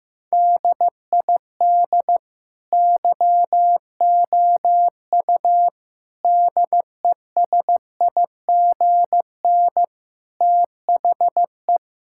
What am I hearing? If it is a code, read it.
Morse code, heard as a keyed tone, about 15 wpm, 700 Hz. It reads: DID YOU DESIGN THE
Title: Sentences from Top 1000 Words - Review Part 5 - 15wpm